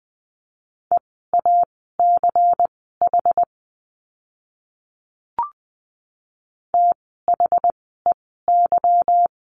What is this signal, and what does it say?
Morse code, heard as a keyed tone, about 20 wpm, spaced as for 14 wpm, 700 Hz. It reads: EACH THEY
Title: 192 Lesson 31f - Letter v - Words 20wpm